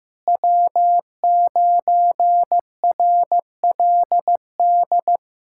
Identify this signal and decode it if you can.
Morse code, heard as a keyed tone, about 15 wpm, 700 Hz. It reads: W9RLD
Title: USA Callsign DE Callsign 15wpm